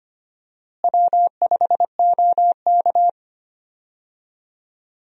Morse code, heard as a keyed tone, about 25 wpm, 700 Hz. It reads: W5OK
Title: USA Call Signs - 4 Characters - Repeated 2x - 25wpm